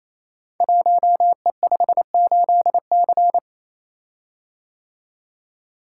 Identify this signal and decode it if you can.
Morse code, heard as a keyed tone, about 27 wpm, 700 Hz. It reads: 1E58C
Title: Code Groups - 5 Letter:Numbers - 28wpm